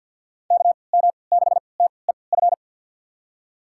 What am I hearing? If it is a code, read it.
Morse code, heard as a keyed tone, about 47 wpm, spaced as for 28 wpm, 700 Hz. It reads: KM6TEF
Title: USA Call Signs - 6 Characters - 3x Character Spacing - 50wpm